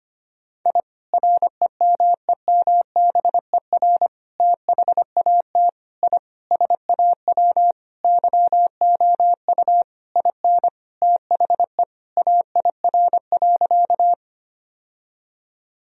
Morse code, heard as a keyed tone, about 25 wpm, 700 Hz. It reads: I REMEMBER THAT I SAW YOU IN THE AIR.
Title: Sentences from Top 1000 Words - Review Part 1 - 25wpm